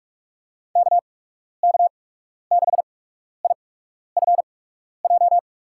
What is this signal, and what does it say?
Morse code, heard as a keyed tone, about 47 wpm, spaced as for 12 wpm, 700 Hz. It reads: KK6IFJ
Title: USA Call Signs - 6 Characters - 8x Character Spacing - 45wpm